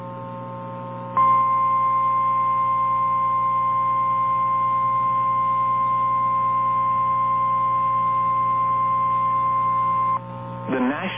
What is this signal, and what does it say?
Emergency Alert System (SAME) tones (the 1050 Hz attention tone), with speech. It is heard from 1.1 to 10.2 s.
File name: eas / KNCYKBIE / KNCYKBIE